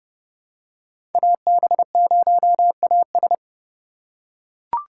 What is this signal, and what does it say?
Morse code, heard as a keyed tone, about 30 wpm, 700 Hz. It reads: AB0AS
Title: USA Call Signs - 3x Character Spacing - 30wpm